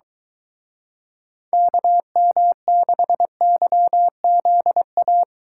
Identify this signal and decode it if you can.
Morse code, heard as a keyed tone, about 22 wpm, 700 Hz. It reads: KM6YZA